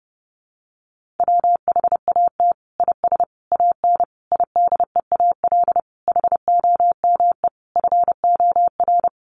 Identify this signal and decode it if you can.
Morse code, heard as a keyed tone, about 30 wpm, 700 Hz. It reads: WHAT IS AN IDEAL HOME FOR
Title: Sentences from Top 1400 Words 30wpm